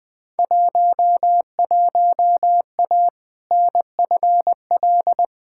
Morse code, heard as a keyed tone, about 20 wpm, 700 Hz. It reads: 11A NFL
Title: Field Day Exchanges Encore - 20wpm